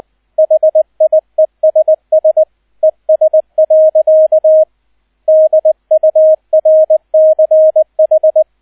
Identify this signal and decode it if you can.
Morse code, heard as a keyed tone, about 20 wpm, 630 Hz, with radio noise. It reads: HIESS ES. DURCH
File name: 002 Zweiter Streich 20wpm